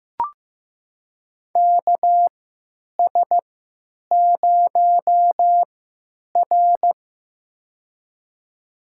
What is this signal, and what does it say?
Morse code, heard as a keyed tone, about 15 wpm, spaced as for 9 wpm, 700 Hz. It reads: KS0R
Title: USA Call Signs - 4 Characters - 3x Character Spacing - 15wpm